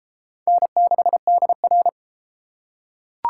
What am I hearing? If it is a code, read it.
Morse code, heard as a keyed tone, about 31 wpm, 700 Hz. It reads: N6DR